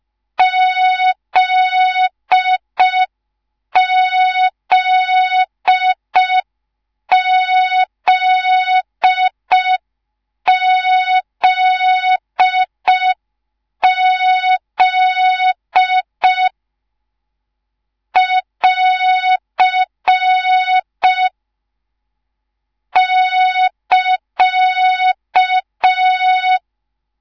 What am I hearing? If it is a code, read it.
Morse code, heard as a keyed tone, about 5 wpm, 3700 Hz. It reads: ZZZZZ <AR> <KA>